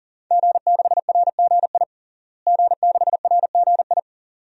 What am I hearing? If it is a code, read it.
Morse code, heard as a keyed tone, about 40 wpm, 700 Hz. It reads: G6RGI G6RGI
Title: DXCC-100 Call Signs - 5 Characters - Repeated 2x - 40wpm